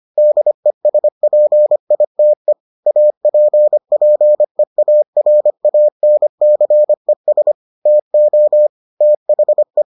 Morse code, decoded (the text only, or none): DESPITE APPEARANCES TO THE